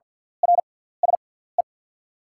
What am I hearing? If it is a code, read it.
Morse code, heard as a keyed tone, about 50 wpm, spaced as for 17 wpm, 690 Hz. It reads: RSE
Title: 6x Character Spacing - Top 300 Words - 50wpm